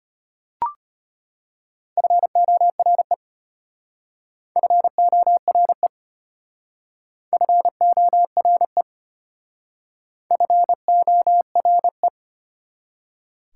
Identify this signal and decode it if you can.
Morse code, heard as a keyed tone, about 32 wpm, 690 Hz. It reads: FORE FORE FORE FORE